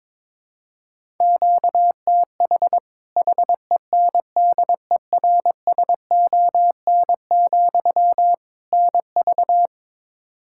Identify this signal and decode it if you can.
Morse code, heard as a keyed tone, about 22 wpm, 700 Hz. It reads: QTH HENDERSON, NV
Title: Practice QSO - Exchange 1 - Beginning - 22wpm